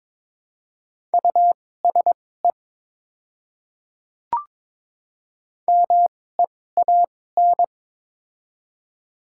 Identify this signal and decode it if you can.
Morse code, heard as a keyed tone, about 22 wpm, spaced as for 16 wpm, 700 Hz. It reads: USE MEAN
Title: 2x Character Spacing - Top 200 Words - 22wpm